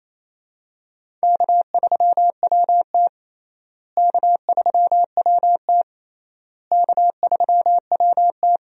Morse code, heard as a keyed tone, about 27 wpm, 700 Hz. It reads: K3WT K3WT K3WT
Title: USA Call Signs - 4 Characters - Repeated 3x - 28wpm